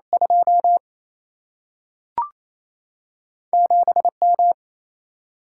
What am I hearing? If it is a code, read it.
Morse code, heard as a keyed tone, about 28 wpm, 700 Hz. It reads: 2 7M
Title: Single-Letter Single-Number 28wpm